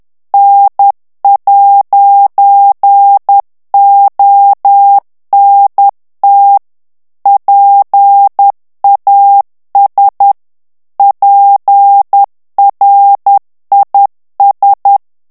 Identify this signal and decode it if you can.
Morse code, heard as a keyed tone, about 10 wpm, 800 Hz. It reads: N'ONT PAS PRIS
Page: Français - texte en clair 1 - vitesse : 10 mots minute